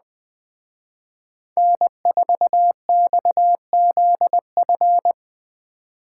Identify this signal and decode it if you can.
Morse code, heard as a keyed tone, about 20 wpm, 700 Hz. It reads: N4XZF
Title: Speed Racing - USA 1x3 Call Signs 17wpm